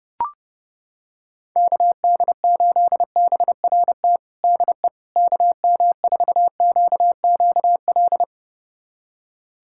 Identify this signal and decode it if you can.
Morse code, heard as a keyed tone, about 32 wpm, 700 Hz. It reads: KD8BRT DE KM4QQL
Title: USA Callsign DE Callsign 30wpm